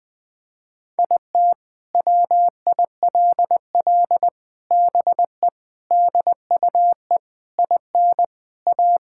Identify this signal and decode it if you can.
Morse code, heard as a keyed tone, about 20 wpm, 700 Hz. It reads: IT WILL BE DUE IN A
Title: Sentences from Top 1000 Words - Review Part 6 - 20wpm